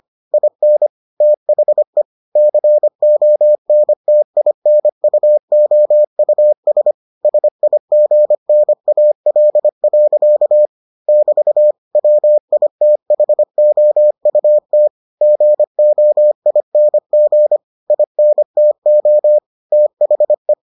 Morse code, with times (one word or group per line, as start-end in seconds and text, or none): 0.3-0.9 IN
1.2-2.0 THE
2.4-6.9 CONTINUOUS
7.2-10.7 SIGNAL.
11.1-11.7 <BT>
11.9-14.9 WITHOUT
15.2-17.6 GOING
17.9-19.4 INTO
19.7-20.5 THE